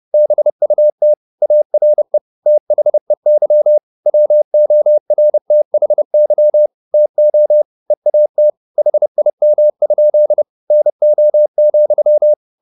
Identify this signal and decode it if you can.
Morse code, heard as a keyed tone, about 30 wpm, 600 Hz. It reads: BUT ARE THEY WORTHY TO EAT HIM? NO,